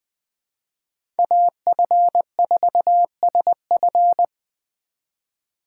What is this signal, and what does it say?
Morse code, heard as a keyed tone, about 20 wpm, 700 Hz. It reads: AF4SF
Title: USA Call Signs - 5 Characters - 3x Character Spacing - 20wpm